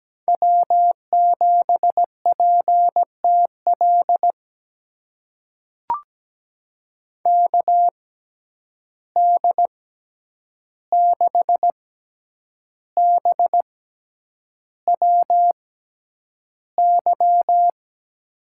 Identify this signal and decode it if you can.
Morse code, heard as a keyed tone, about 17 wpm, spaced as for 8 wpm, 700 Hz. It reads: W7PTL KD6BWY
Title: USA Call Signs - 6x Character Spacing - 17wpm